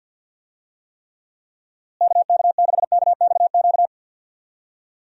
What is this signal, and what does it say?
Morse code, heard as a keyed tone, about 53 wpm, 700 Hz. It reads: KK6KXQ
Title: USA Call Signs - 6 Characters - Repeated 3x - 50wpm